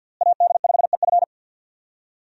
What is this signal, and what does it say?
Morse code, heard as a keyed tone, about 53 wpm, 700 Hz. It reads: AD5EF
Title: USA Call Signs - 5 Characters - Repeated 2x - 50wpm